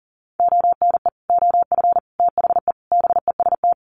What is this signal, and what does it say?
Morse code, heard as a keyed tone, about 40 wpm, 700 Hz. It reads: ONE OF THE BEST
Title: Sets of 4 Words 40wpm